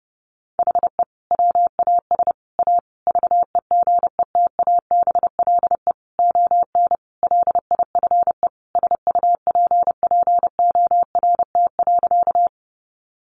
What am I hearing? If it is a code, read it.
Morse code, heard as a keyed tone, about 30 wpm, 700 Hz. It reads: HE WAS A VEGETABLE ON LIFE SUPPORT.
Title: Sentences from Top 1300 Words 30wpm